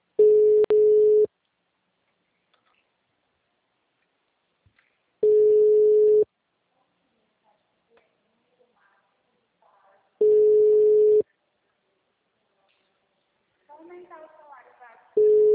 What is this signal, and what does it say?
Ringback tone (Europe, 425 Hz), with speech.